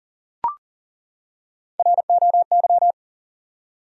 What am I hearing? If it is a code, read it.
Morse code, heard as a keyed tone, about 40 wpm, 700 Hz. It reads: ROY
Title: USA Common Names - 40wpm